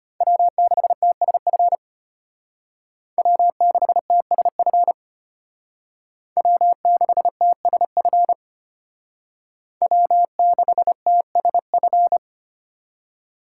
Morse code, heard as a keyed tone, about 30 wpm, 700 Hz. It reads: W6TSF W6TSF W6TSF W6TSF
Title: Speed Racing - USA Call Signs 25wpm